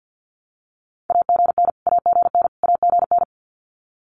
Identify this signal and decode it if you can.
Morse code, heard as a keyed tone, about 50 wpm, 700 Hz. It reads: AGN AGN AGN
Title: Abbreviations Repeated 3x - 50wpm